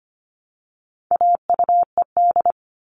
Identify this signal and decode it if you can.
Morse code, heard as a keyed tone, about 25 wpm, 700 Hz. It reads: AUED
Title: All Cut Numbers - 4-Digit 25wpm